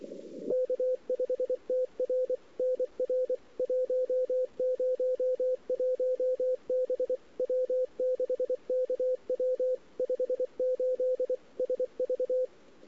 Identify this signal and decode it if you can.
Morse code, heard as a keyed tone, about 24 wpm, 510 Hz. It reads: K5TR NR 101 B W6KW 58 SV
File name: CW / 2008-ss-cw-k5tr-op-n5rz -02:00